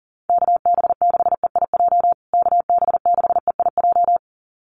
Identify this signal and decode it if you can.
Morse code, heard as a keyed tone, about 40 wpm, 700 Hz. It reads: KB6EIJ KB6EIJ
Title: USA Call Signs Repeated 2x 40wpm